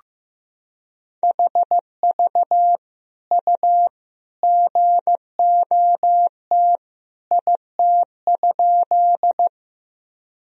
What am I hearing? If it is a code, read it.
Morse code, heard as a keyed tone, about 15 wpm, 700 Hz. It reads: HV U GOT IT?
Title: Sentences from Top 300 Words and Abbreviations 15wpm